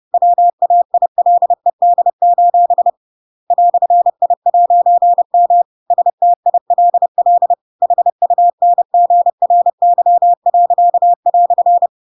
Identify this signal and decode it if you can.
Morse code, heard as a keyed tone, about 30 wpm, 700 Hz. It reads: WAILED: "I'M STILL HUNGRY."
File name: Pinocchio / Pinocchio ch7 30 30 ik1hge